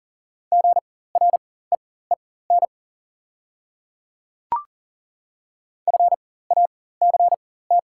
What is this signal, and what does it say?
Morse code, heard as a keyed tone, about 40 wpm, spaced as for 19 wpm, 700 Hz. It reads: GREEN FACT
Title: 4x Character Spacing - Top 500 Words - 40wpm